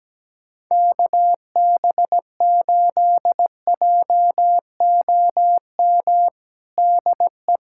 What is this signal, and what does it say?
Morse code, heard as a keyed tone, about 17 wpm, 700 Hz. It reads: KB8JOM DE
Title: USA Callsign DE Callsign - 6 Characters - 17wpm